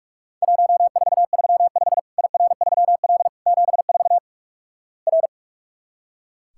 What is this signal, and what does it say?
Morse code, heard as a keyed tone, about 46 wpm, 700 Hz. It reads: 1435 IR3L 74 R
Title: Worked All Europe DX Contest - QTC Exchange - 45wpm